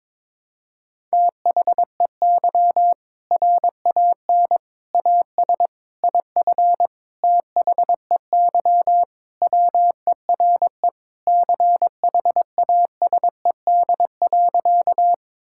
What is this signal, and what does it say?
Morse code, heard as a keyed tone, about 22 wpm, 700 Hz. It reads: THEY RAN AS IF THEY WERE CHASED.
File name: practice / 22wpm / Sentences with 4-Word Phrases - 22wpm